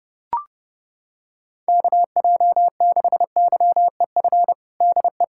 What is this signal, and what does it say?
Morse code, heard as a keyed tone, about 32 wpm, 690 Hz. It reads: KJ6YEF DE
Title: USA Callsign DE Callsign 30wpm